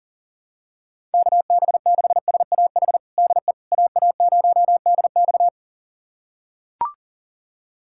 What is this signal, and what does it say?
Morse code, heard as a keyed tone, about 37 wpm, 700 Hz. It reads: KB6SAH DE AA0DX
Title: USA Callsign DE Callsign 40wpm